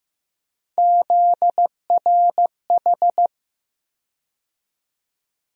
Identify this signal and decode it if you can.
Morse code, heard as a keyed tone, about 16 wpm, 700 Hz. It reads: ZRH
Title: USA Call Sign Postfixes - 3 Letters 15wpm